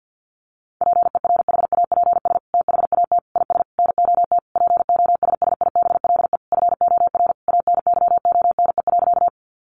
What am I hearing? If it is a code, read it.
Morse code, heard as a keyed tone, about 50 wpm, 700 Hz. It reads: PERHAPS THAT IS NOT POSSIBLE FOR ANYONE.
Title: Sentences from Top 900 Words 50wpm